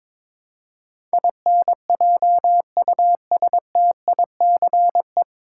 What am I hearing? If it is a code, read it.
Morse code, heard as a keyed tone, about 21 wpm, 700 Hz. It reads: INJUSTICE
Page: Morse Code Ninja practice 22wpm